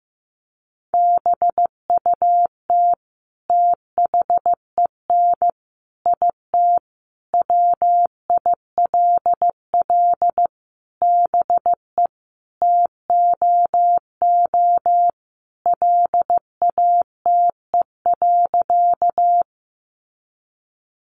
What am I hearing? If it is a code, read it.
Morse code, heard as a keyed tone, about 15 wpm, 700 Hz. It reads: BUT THEN IT WILL BE TOO LATE.
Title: Sentences from Top 300 Words 15wpm